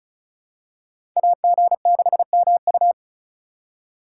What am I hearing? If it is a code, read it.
Morse code, heard as a keyed tone, about 36 wpm, 700 Hz. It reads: AG6MU
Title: USA Call Signs - 5 Characters - 2x Character Spacing - 35wpm